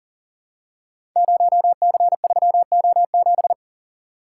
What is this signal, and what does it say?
Morse code, heard as a keyed tone, about 39 wpm, 690 Hz. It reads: 0C3O7
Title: Code Groups - 5 Letter:Numbers - 40wpm